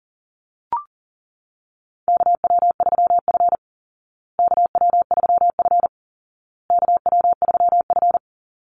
Morse code, heard as a keyed tone, about 40 wpm, 700 Hz. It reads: KW3F KW3F KW3F
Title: Call Signs Repeated 2x - 40wpm